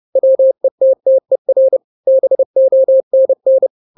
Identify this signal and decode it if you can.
Morse code, heard as a keyed tone, about 30 wpm, 530 Hz. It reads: WETTER BONN
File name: daily / 29wpm